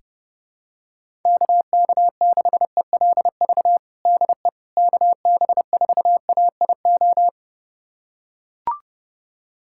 Morse code, heard as a keyed tone, about 31 wpm, 700 Hz. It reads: KK6ELV DE KB4AIO
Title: USA Callsign DE Callsign - 6 Characters - 30wpm